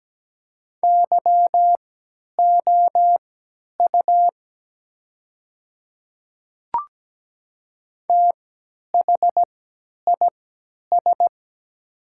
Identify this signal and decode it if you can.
Morse code, heard as a keyed tone, about 17 wpm, spaced as for 10 wpm, 700 Hz. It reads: YOU THIS
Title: 3x Character Spacing - Top 100 Words - 17wpm